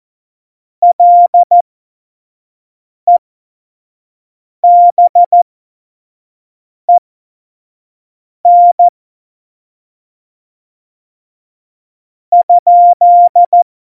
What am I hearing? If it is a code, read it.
Morse code, heard as a keyed tone, about 14 wpm, spaced as for 4 wpm, 700 Hz. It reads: LEBEN?
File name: Morsekurs / Witze / Witz 16. April 2019, 25 F70